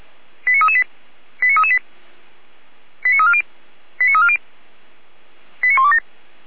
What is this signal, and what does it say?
Five-tone selective call (ZVEI), heard as quick runs of five tones, tones of about 70 ms. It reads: 89338 89338 89340 89340 89127